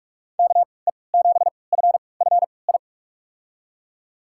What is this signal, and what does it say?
Morse code, heard as a keyed tone, about 46 wpm, spaced as for 26 wpm, 700 Hz. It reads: KE7FFI
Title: USA Call Signs - 3x Character Spacing - 45wpm